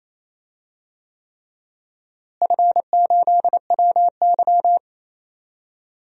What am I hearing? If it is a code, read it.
Morse code, heard as a keyed tone, about 27 wpm, 700 Hz. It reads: F8WY
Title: DXCC-100 Call Signs - 4 Characters - Repeated 3x - 28wpm